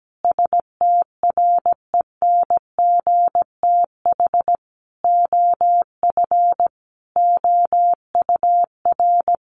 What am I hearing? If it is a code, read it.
Morse code, heard as a keyed tone, about 17 wpm, 700 Hz. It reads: STRENGTH OF OUR
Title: Sentences from Top 1300 Words 17wpm